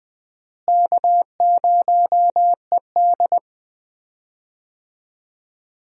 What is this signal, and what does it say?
Morse code, heard as a keyed tone, about 20 wpm, 700 Hz. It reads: K0ED
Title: USA Call Signs - 4 Characters - Repeated 3x - 20wpm